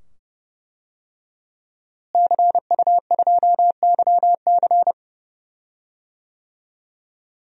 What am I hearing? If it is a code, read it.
Morse code, heard as a keyed tone, about 32 wpm, 700 Hz. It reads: CU2YC
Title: DXCC-100 Call Signs - 5 Characters - Repeated 2x - 30wpm